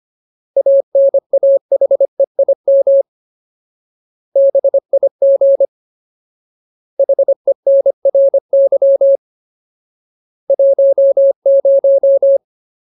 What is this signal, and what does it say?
Morse code, heard as a keyed tone, about 25 wpm, 550 Hz. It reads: ANAHEIM BIG HENRY 10
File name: Ultimate Word List / 25WPM Ultimate